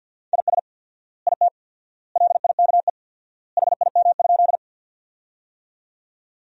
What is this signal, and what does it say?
Morse code, heard as a keyed tone, about 50 wpm, 700 Hz. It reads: IS IT LIKE HIM?
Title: 4x Word-Spacing - Sentences from Top 100 Words - 50wpm